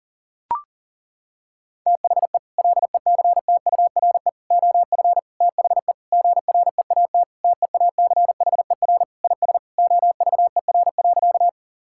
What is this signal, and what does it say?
Morse code, heard as a keyed tone, about 40 wpm, 700 Hz. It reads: THE LECTURE OF THE GREAT TEACHER IS OVER.
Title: Sentences from Top 1600 Words 40wpm